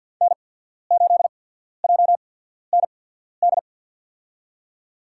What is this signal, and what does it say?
Morse code, heard as a keyed tone, about 53 wpm, spaced as for 14 wpm, 700 Hz. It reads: N8JND
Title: USA Call Signs - 5 Characters - 8x Character Spacing - 50wpm